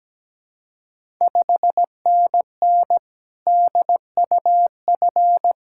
Morse code, heard as a keyed tone, about 17 wpm, 690 Hz. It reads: 5NN DUF
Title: Ontario QSO Party Exchange 17wpm